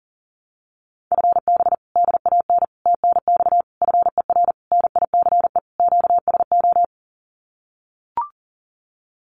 Morse code, heard as a keyed tone, about 40 wpm, 700 Hz. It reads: FB DAN TNX FER NICE QSO
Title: Common QSO Phrases Encore 40wpm